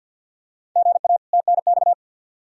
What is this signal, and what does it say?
Morse code, heard as a keyed tone, about 49 wpm, 700 Hz. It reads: GA TNX
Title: Practice QSO - Exchange 1 - Beginning - 50wpm